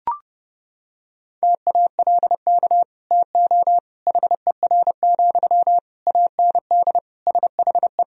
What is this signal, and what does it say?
Morse code, heard as a keyed tone, about 32 wpm, 700 Hz. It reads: TALK TO HER, AND SHE
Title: Sentences from Top 1000 Words - Review Part 3 - 30wpm